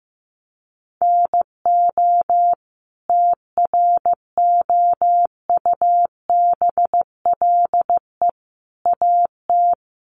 Morse code, heard as a keyed tone, about 15 wpm, 700 Hz. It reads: NO TROUBLE AT
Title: Sentences from Top 1200 Words 15wpm